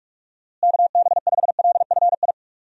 Morse code, heard as a keyed tone, about 41 wpm, 700 Hz. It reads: KB5LFI
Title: USA Call Signs - 6 Characters - Repeated 3x - 45wpm